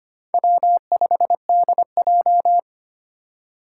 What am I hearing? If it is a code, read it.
Morse code, heard as a keyed tone, about 25 wpm, 700 Hz. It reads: W5DJ
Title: Four Challenging Letter-Number Combinations 25wpm